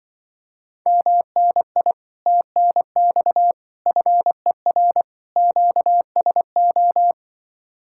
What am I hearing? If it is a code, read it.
Morse code, heard as a keyed tone, about 24 wpm, 700 Hz. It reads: MNI TNX FER QSO
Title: Speed-Racing Common QSO Phrases - 20wpm